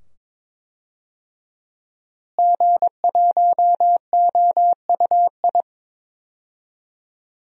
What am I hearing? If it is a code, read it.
Morse code, heard as a keyed tone, about 22 wpm, 700 Hz. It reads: G1OUI
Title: DXCC-100 Call Signs - 5 Characters - Repeated 2x - 22wpm